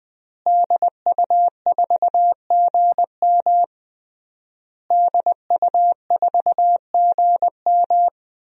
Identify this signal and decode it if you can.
Morse code, heard as a keyed tone, about 20 wpm, 700 Hz. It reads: DU4GM DU4GM
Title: DXCC-100 Call Signs - 5 Characters - Repeated 3x - 20wpm